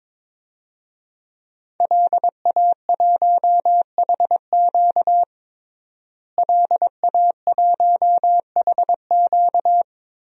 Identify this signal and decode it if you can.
Morse code, heard as a keyed tone, about 21 wpm, 700 Hz. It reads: LA1HQ LA1HQ
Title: DXCC-100 Call Signs - 5 Characters - Repeated 3x - 22wpm